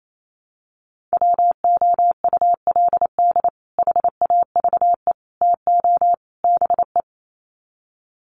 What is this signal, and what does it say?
Morse code, heard as a keyed tone, about 28 wpm, 700 Hz. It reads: WOULD HAVE TO BE
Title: Sets of 4 Words 28wpm